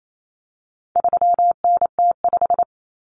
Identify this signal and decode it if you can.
Morse code, heard as a keyed tone, about 28 wpm, 700 Hz. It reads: 3NT5
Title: Common Cut Numbers - 4-Digit 28wpm